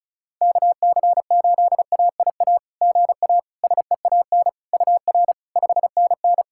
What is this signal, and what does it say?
Morse code, heard as a keyed tone, about 36 wpm, 690 Hz. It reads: KC8AIA GA SEAN UR 5NN
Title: SOTA QSO Practice 35wpm